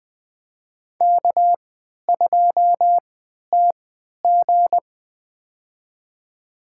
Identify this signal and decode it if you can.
Morse code, heard as a keyed tone, about 20 wpm, spaced as for 11 wpm, 700 Hz. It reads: K2TG
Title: USA Call Signs - 4 Characters - 3x Character Spacing - 20wpm